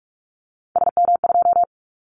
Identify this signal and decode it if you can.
Morse code, heard as a keyed tone, about 44 wpm, 700 Hz. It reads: SM2
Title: Callsign Prefixes for Top 25 DXCC - Repeated 3x - 45wpm